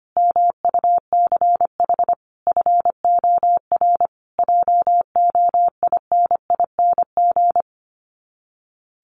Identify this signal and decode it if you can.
Morse code, heard as a keyed tone, about 25 wpm, 700 Hz. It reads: MUCH FOR JOINING
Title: Sets of 5 Words Encore2 25wpm